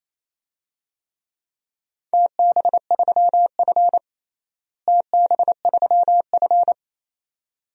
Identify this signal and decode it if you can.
Morse code, heard as a keyed tone, about 28 wpm, 700 Hz. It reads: TB3F TB3F
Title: DXCC-100 Call Signs - 4 Characters - Repeated 2x - 28wpm